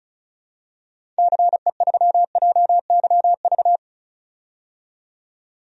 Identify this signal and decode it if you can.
Morse code, heard as a keyed tone, about 36 wpm, 700 Hz. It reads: CE3JYV
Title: DXCC-100 Call Signs - 6 Characters - 35wpm